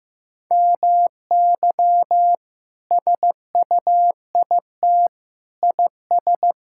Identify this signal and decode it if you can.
Morse code, heard as a keyed tone, about 15 wpm, 700 Hz. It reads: MY SUIT IS
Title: Sentences from Top 1600 Words 15wpm